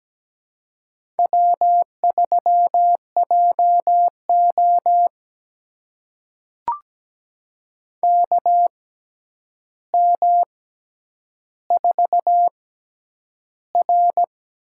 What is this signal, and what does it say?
Morse code, heard as a keyed tone, about 17 wpm, spaced as for 9 wpm, 700 Hz. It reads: W3JO KM4R
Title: USA Call Signs - 6x Character Spacing - 17wpm